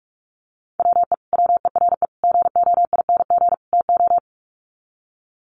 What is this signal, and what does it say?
Morse code, heard as a keyed tone, about 44 wpm, 700 Hz. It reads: WE WERE GOING TO
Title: Sets of 4 Words 45wpm